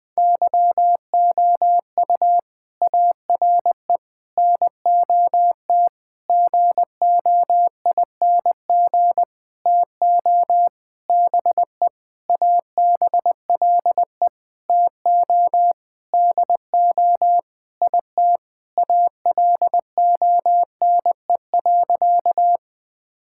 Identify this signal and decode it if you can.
Morse code, heard as a keyed tone, about 21 wpm, 700 Hz. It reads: YOU ARE NOT GOING TO BE ABLE TO DO IT ALONE.
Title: Sentences with 7-Word Phrases - Spoken Beginning - 20wpm